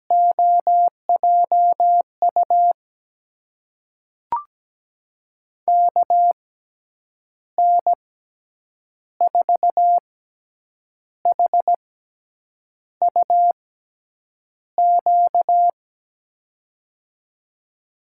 Morse code, heard as a keyed tone, about 17 wpm, spaced as for 7 wpm, 700 Hz. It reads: OJU KN4HUQ
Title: USA Call Signs - 6x Character Spacing - 17wpm